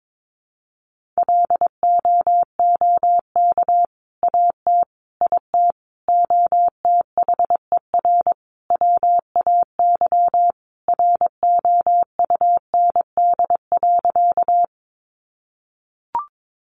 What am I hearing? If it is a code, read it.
Morse code, heard as a keyed tone, about 22 wpm, 700 Hz. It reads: LOOK AT IT OTHER WAY ROUND.
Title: Sentences from Top 200 Words 22wpm